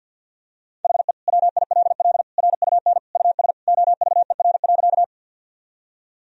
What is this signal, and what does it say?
Morse code, heard as a keyed tone, about 51 wpm, 700 Hz. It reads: HE WILL RUN US OVER.
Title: Sentences from Top 1000 Words - Review Part 1 - 50wpm